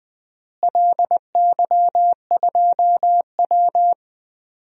Morse code, heard as a keyed tone, about 20 wpm, 700 Hz. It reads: LY2W
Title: DXCC-100 Call Signs - 4 Characters - Repeated 2x - 20wpm